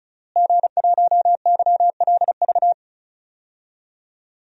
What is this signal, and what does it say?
Morse code, heard as a keyed tone, about 34 wpm, 700 Hz. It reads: G1YLV
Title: DXCC-100 Call Signs - 5 Characters - Repeated 3x - 35wpm